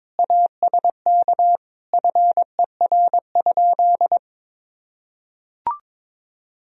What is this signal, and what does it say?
Morse code, heard as a keyed tone, about 22 wpm, 700 Hz. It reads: ASK FER?
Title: Sentences from Top 200 Words and Abbreviations 22wpm